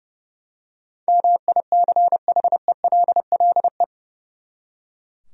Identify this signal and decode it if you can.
Morse code, heard as a keyed tone, about 32 wpm, 700 Hz. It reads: MICHELLE
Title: USA Common Names - 30wpm